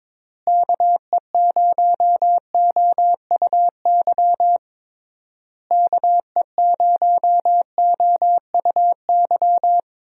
Morse code, heard as a keyed tone, about 22 wpm, 700 Hz. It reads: KE0OUY KE0OUY
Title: USA Call Signs - 6 Characters - Repeated 3x - 22wpm